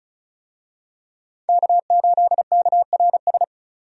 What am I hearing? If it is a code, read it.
Morse code, heard as a keyed tone, about 36 wpm, 700 Hz. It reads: K8KRS
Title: USA Call Signs - 5 Characters - 35wpm